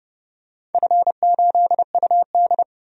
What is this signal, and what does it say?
Morse code, heard as a keyed tone, about 30 wpm, 700 Hz. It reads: F8UD
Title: DXCC-100 Call Signs - 4 Characters - 30wpm